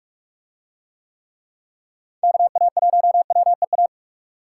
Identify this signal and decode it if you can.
Morse code, heard as a keyed tone, about 46 wpm, 700 Hz. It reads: KA1WEA